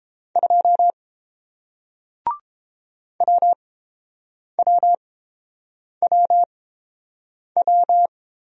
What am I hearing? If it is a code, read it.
Morse code, heard as a keyed tone, about 30 wpm, spaced as for 7 wpm, 700 Hz. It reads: 2 WWWW